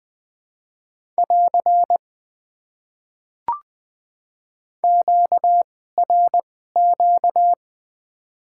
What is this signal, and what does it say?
Morse code, heard as a keyed tone, about 20 wpm, spaced as for 15 wpm, 700 Hz. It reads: <AR> QRQ
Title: Top 90 QSO Elements - 2x Character Spacing - 20wpm